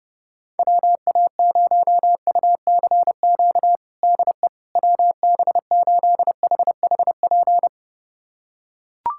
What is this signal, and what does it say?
Morse code, heard as a keyed tone, about 30 wpm, 700 Hz. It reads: WA0UCQ DE WB8HHP
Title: USA Callsign DE Callsign - 6 Characters - 30wpm